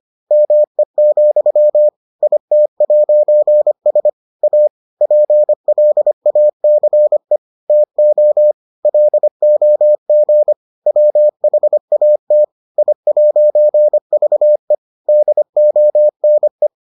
Morse code, heard as a keyed tone, about 25 wpm, 600 Hz. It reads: ME, IT'S A PLACE TO LOG WHAT I'VE DONE